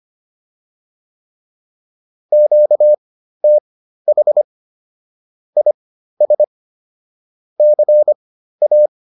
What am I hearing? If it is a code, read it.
Morse code, heard as a keyed tone, about 25 wpm, spaced as for 13 wpm, 600 Hz. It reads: QTH IS CA